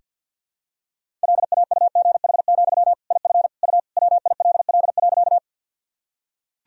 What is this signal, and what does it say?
Morse code, heard as a keyed tone, about 47 wpm, 700 Hz. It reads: LAUGH, IF U WILL.
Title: Sentences from Top 500 Words and Abbreviations 50wpm